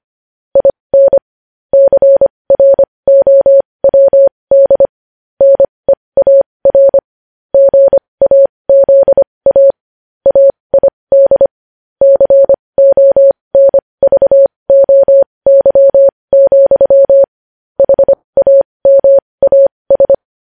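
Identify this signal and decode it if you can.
Morse code, heard as a keyed tone, about 25 wpm, spaced as for 20 wpm, 550 Hz. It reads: IN CROWD NEAR GAZA AID CONVOY, HAMAS
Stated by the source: Morse code news headlines sent at 20 wpm.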